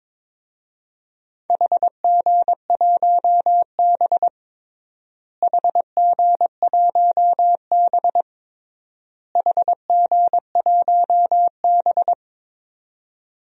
Morse code, heard as a keyed tone, about 22 wpm, 700 Hz. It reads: HG1B HG1B HG1B
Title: DXCC-100 Call Signs - 4 Characters - Repeated 3x - 22wpm